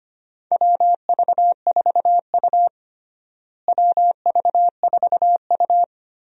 Morse code, heard as a keyed tone, about 25 wpm, 700 Hz. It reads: WV4U WV4U
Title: USA Call Signs - 4 Characters - Repeated 3x - 25wpm